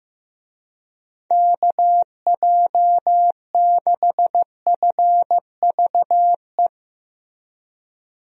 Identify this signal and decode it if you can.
Morse code, heard as a keyed tone, about 15 wpm, 700 Hz. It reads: KJ6FVE
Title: USA Call Signs - 6 Characters - Repeated 3x - 15wpm